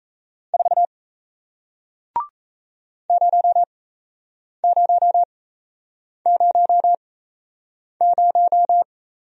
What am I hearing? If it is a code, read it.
Morse code, heard as a keyed tone, about 38 wpm, spaced as for 8 wpm, 700 Hz. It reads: 4 0000